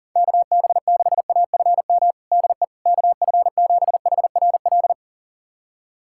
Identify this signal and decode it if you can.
Morse code, heard as a keyed tone, about 40 wpm, 700 Hz. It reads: KB6AFM DE KF7HRL
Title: USA Callsign DE Callsign - 6 Characters - 40wpm